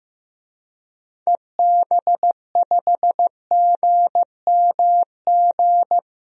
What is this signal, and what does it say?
Morse code, heard as a keyed tone, about 15 wpm, 700 Hz. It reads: EB5GMG
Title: DXCC-100 Call Signs - 6 Characters - Repeated 3x - 15wpm